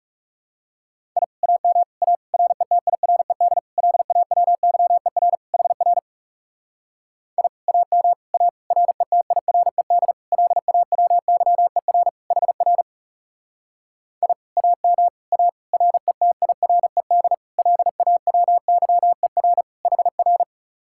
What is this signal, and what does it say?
Morse code, heard as a keyed tone, about 40 wpm, 700 Hz. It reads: I AM A RETIRED LAWYER HR I AM A RETIRED LAWYER HR I AM A RETIRED LAWYER HR